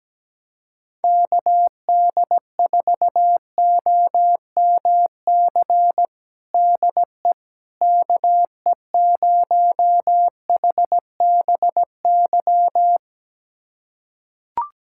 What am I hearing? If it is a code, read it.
Morse code, heard as a keyed tone, about 17 wpm, 700 Hz. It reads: KD4OMC DE KE0HBY
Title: USA Callsign DE Callsign - 6 Characters - 17wpm